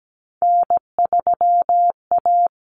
Morse code, heard as a keyed tone, about 17 wpm, 700 Hz. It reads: N3A
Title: Common Cut Numbers - 3-Digit 17wpm